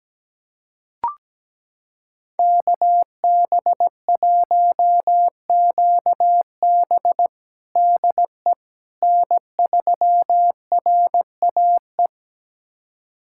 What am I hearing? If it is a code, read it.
Morse code, heard as a keyed tone, about 17 wpm, 700 Hz. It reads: KB1QB DE N3RAE
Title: USA Callsign DE Callsign - 5 Characters - 17wpm